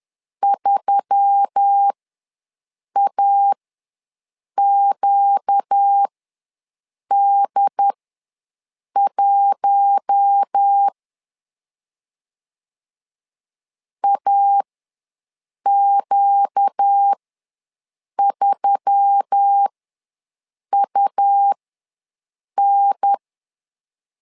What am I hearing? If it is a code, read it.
Morse code, heard as a keyed tone, about 10 wpm, spaced as for 6 wpm, 800 Hz. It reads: 3AQD1 AQ3UN
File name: LEZIONI / 4 Q1S3 ws 10wpm 2437